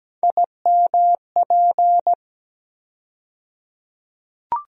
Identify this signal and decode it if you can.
Morse code, heard as a keyed tone, about 17 wpm, 700 Hz. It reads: IMP
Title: USA Call Sign Postfixes - 3 Letters 17wpm